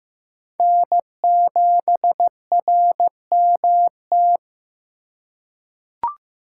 Morse code, heard as a keyed tone, about 15 wpm, 700 Hz. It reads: N7RMT
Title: USA Call Signs - 5 Characters - 4x Character Spacing - 15wpm